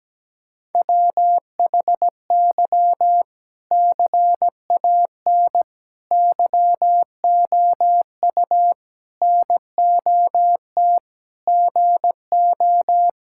Morse code, heard as a keyed tone, about 17 wpm, 700 Hz. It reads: WHY CAN YOU NOT GO
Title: Sentences from Top 1600 Words 17wpm